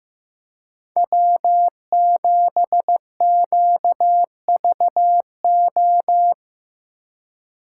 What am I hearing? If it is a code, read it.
Morse code, heard as a keyed tone, about 18 wpm, 700 Hz. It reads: W7QVO
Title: Speed Racing - USA 1x3 Call Signs 15wpm